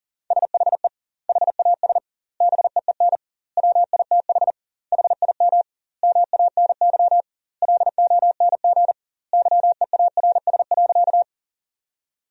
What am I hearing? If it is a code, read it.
Morse code, heard as a keyed tone, about 37 wpm, 700 Hz. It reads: SHE HAS BEEN WITH HIM MANY LONG YEARS.
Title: 2x Word-Spacing - Sentences from Top 200 Words - 40wpm